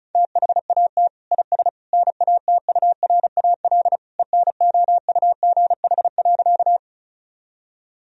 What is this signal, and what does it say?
Morse code, heard as a keyed tone, about 34 wpm, 700 Hz. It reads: THAT IS NATURAL ENOUGH.
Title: Sentences from Top 1000 Words - Review Part 4 - 35wpm